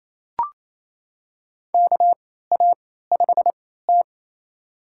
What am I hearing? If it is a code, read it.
Morse code, heard as a keyed tone, about 28 wpm, spaced as for 16 wpm, 700 Hz. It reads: KA5T
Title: USA Call Signs - 3x Character Spacing - 28wpm